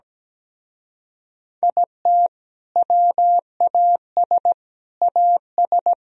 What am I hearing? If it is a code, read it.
Morse code, heard as a keyed tone, about 18 wpm, 700 Hz. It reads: IT WAS AS
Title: Speed-Racing - Sets of 4 Words 17wpm